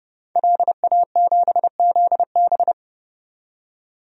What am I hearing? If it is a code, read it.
Morse code, heard as a keyed tone, about 29 wpm, 690 Hz. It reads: LA7ZB
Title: DXCC-100 Call Signs - 5 Characters - 30wpm